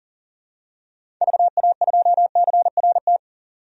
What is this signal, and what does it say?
Morse code, heard as a keyed tone, about 40 wpm, 700 Hz. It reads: VA2CRT
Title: DXCC-100 Call Signs - 6 Characters - 40wpm